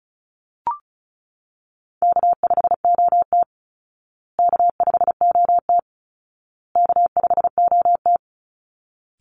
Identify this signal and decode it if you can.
Morse code, heard as a keyed tone, about 34 wpm, 700 Hz. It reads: K5OT K5OT K5OT
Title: Easy Call Signs Repeated 3x - 35wpm